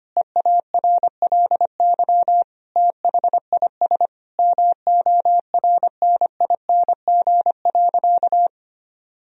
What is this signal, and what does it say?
Morse code, heard as a keyed tone, about 25 wpm, 700 Hz. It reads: EARLY THIS MORNING.
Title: Sentences from Top 1000 Words - Review Part 2 - 25wpm